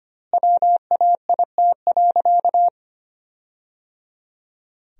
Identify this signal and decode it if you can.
Morse code, heard as a keyed tone, about 25 wpm, 700 Hz. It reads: WAIT.
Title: 2x Word-Spacing - Sentences from Top 500 Words - 25wpm